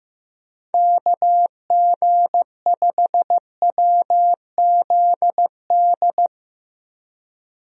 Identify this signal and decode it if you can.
Morse code, heard as a keyed tone, about 15 wpm, 700 Hz. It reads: KG5WZD
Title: USA Call Signs - 6 Characters - Repeated 2x - 15wpm